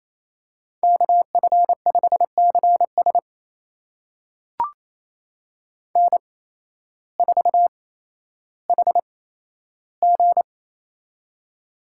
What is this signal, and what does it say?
Morse code, heard as a keyed tone, about 28 wpm, spaced as for 13 wpm, 700 Hz. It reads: KF5CS N4HG
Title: USA Call Signs - 8x Character Spacing - 28wpm